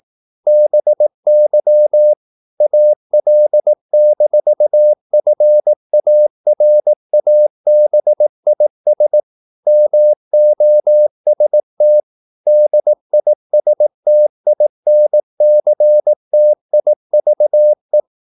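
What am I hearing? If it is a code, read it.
Morse code, heard as a keyed tone, about 18 wpm, 600 Hz. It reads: BY AL-FARABIS MOST DISTINCTIVE
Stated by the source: Morse practice at 18 wpm, English text, from CW med Gnister.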